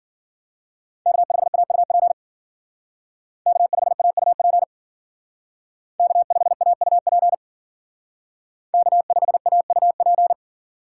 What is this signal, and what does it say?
Morse code, heard as a keyed tone, about 48 wpm, 700 Hz. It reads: K5AUP K5AUP K5AUP K5AUP
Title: Speed Racing - USA 1x3 Call Signs 40wpm